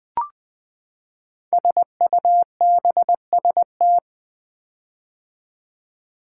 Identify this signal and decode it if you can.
Morse code, heard as a keyed tone, about 20 wpm, 700 Hz. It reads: SUBST
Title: Build A Word - 5 Letters At A Time - 10-Letter Words - 20wpm